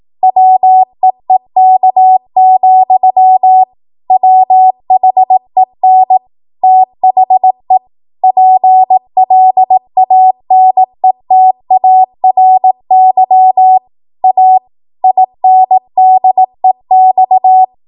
Morse code, heard as a keyed tone, about 18 wpm, 750 Hz. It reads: WEEK, WHEN THE PLANETARY A INDEX
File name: Archive / 18 WPM / 18WPM